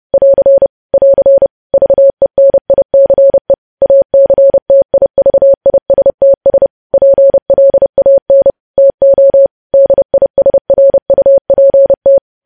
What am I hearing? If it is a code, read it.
Morse code, heard as a keyed tone, about 30 wpm, 550 Hz. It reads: <AR> <AR> VENICE ACTIVISTS PLAN TO DISRUPT
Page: news headlines in morse code at 30 wpm